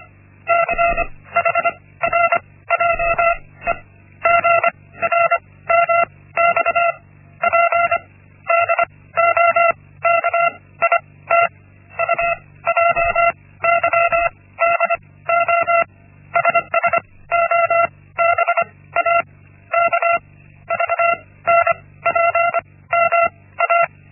Morse code, heard as a keyed tone, about 25 wpm, spaced as for 16 wpm, 680 Hz. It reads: CHRJE GRMXP DOKIT UJYDO SSOBA KVNPM A
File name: Linksmakalnis-1